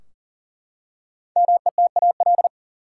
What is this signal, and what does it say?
Morse code, heard as a keyed tone, about 40 wpm, 700 Hz. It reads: METAL
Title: Five Letter Words - 40wpm